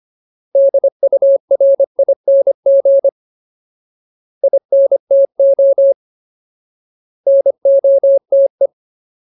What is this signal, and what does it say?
Morse code, heard as a keyed tone, about 25 wpm, 550 Hz. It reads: DURING INTO NOTE
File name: Ultimate Word List / 25WPM Ultimate